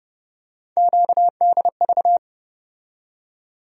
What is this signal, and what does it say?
Morse code, heard as a keyed tone, about 30 wpm, 700 Hz. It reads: QDV
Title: USA Call Sign Postfixes - 3 Letters 30wpm